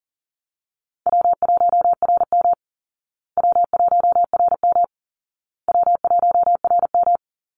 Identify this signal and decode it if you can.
Morse code, heard as a keyed tone, about 40 wpm, 700 Hz. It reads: W1RM W1RM W1RM
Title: Easy Call Signs Repeated 3x - 40wpm